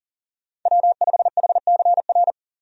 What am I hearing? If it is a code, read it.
Morse code, heard as a keyed tone, about 40 wpm, 700 Hz. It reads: W5HCR
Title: USA Call Signs - 5 Characters - 40wpm